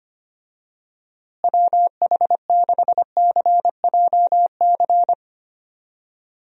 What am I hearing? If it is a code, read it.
Morse code, heard as a keyed tone, about 25 wpm, 700 Hz. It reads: WH6CJC
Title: USA Call Signs - 6 Characters - Repeated 3x - 25wpm